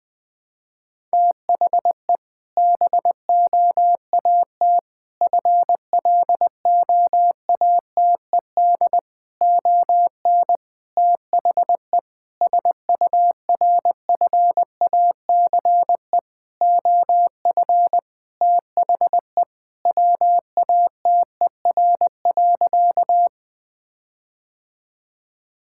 Morse code, heard as a keyed tone, about 21 wpm, 700 Hz. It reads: THE BOAT FLOATED ON THE SURFACE OF THE WATER.
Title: Sentences with 4-Word Phrases - 20wpm